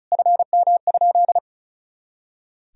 Morse code, heard as a keyed tone, about 34 wpm, 690 Hz. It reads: FM?
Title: Sentences from Top 200 Words and Abbreviations 35wpm